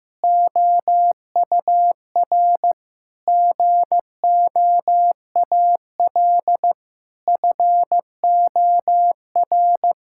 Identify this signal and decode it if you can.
Morse code, heard as a keyed tone, about 15 wpm, 700 Hz. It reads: OUR GOAL FOR
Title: Sentences from Top 1000 Words - Review Part 5 - 15wpm